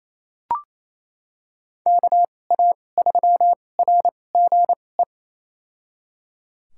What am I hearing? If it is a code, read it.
Morse code, heard as a keyed tone, about 28 wpm, spaced as for 20 wpm, 700 Hz. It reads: KA3RGE
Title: USA Call Signs - 6 Characters - 2x Character Spacing - 28wpm